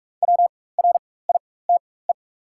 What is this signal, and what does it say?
Morse code, heard as a keyed tone, about 44 wpm, spaced as for 21 wpm, 690 Hz. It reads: WRITE
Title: 4x Character Spacing - Top 200 Words - 45wpm